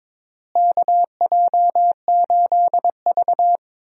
Morse code, heard as a keyed tone, about 22 wpm, 700 Hz. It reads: KJ8V
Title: USA Call Signs - 4 Characters - Repeated 3x - 22wpm